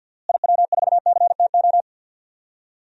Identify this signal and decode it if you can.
Morse code, heard as a keyed tone, about 52 wpm, 700 Hz. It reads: IW4CTO
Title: DXCC-100 Call Signs - 6 Characters - Repeated 3x - 50wpm